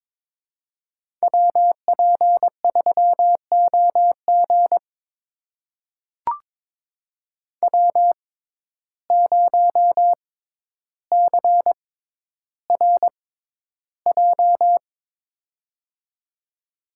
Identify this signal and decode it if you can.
Morse code, heard as a keyed tone, about 22 wpm, spaced as for 11 wpm, 700 Hz. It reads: WP3OG W0CRJ
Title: USA Call Signs - 5 Characters - 6x Character Spacing - 22wpm